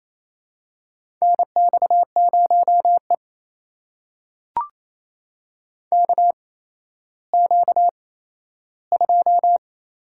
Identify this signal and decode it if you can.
Morse code, heard as a keyed tone, about 28 wpm, spaced as for 14 wpm, 700 Hz. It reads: NX0E KQ2
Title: USA Call Signs - 4 Characters - 8x Character Spacing - 28wpm